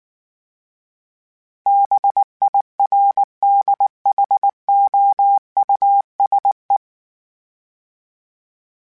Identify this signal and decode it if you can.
Morse code, heard as a keyed tone, about 19 wpm, 800 Hz. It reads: BIRDHOUSE